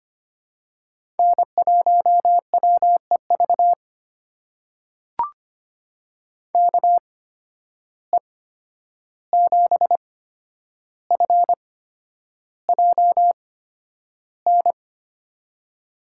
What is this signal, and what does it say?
Morse code, heard as a keyed tone, about 25 wpm, spaced as for 10 wpm, 700 Hz. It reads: N1WEV KE7FJN
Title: USA Call Signs - 8x Character Spacing - 25wpm